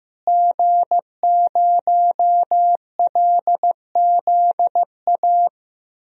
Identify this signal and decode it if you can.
Morse code, heard as a keyed tone, about 15 wpm, 700 Hz. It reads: G0LZA
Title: DXCC-100 Call Signs - 5 Characters - Repeated 2x - 15wpm